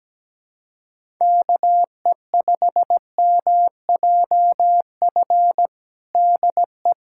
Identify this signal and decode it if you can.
Morse code, heard as a keyed tone, about 17 wpm, 700 Hz. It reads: KE5MJF DE
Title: USA Callsign DE Callsign - 6 Characters - 17wpm